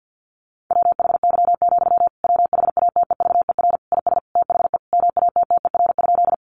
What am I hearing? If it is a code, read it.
Morse code, heard as a keyed tone, about 50 wpm, 700 Hz. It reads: WHY, WHATEVER IS THE MATTER?
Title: Sentences from Top 1100 Words 50wpm